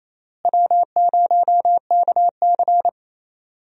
Morse code, heard as a keyed tone, about 28 wpm, 700 Hz. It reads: W0KC
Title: USA Call Signs - 4 Characters - Repeated 3x - 28wpm